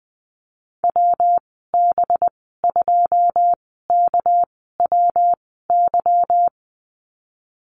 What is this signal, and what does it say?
Morse code, heard as a keyed tone, about 20 wpm, spaced as for 14 wpm, 700 Hz. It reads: WB2KWY
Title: 207 Lesson 33g - Letter j - Call Signs 20wpm